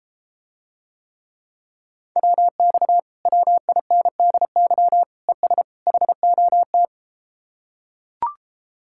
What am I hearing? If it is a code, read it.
Morse code, heard as a keyed tone, about 33 wpm, 700 Hz. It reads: WX WINDY ES HOT